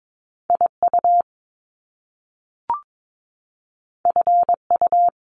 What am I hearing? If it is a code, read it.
Morse code, heard as a keyed tone, about 22 wpm, 700 Hz. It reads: IU FU
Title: Top Two Letter Combinations in Words 22wpm